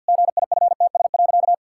Morse code, heard as a keyed tone, about 50 wpm, 700 Hz. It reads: GIFTS.
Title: Pangram Challenge v2 - 50wpm